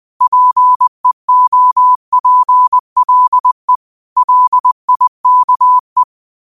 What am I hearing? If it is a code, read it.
Morse code, heard as a keyed tone, about 20 wpm, 1000 Hz. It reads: PEOPLE LIKE